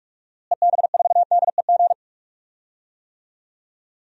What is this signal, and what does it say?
Morse code, heard as a keyed tone, about 46 wpm, 700 Hz. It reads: EB4DEG
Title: DXCC-100 Call Signs - 6 Characters - Repeated 2x - 45wpm